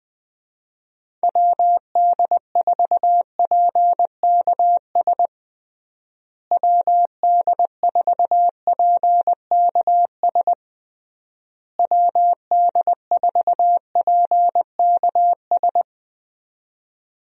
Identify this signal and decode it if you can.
Morse code, heard as a keyed tone, about 20 wpm, 700 Hz. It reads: WD4PKS WD4PKS WD4PKS
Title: USA Call Signs - 6 Characters - Repeated 3x - 20wpm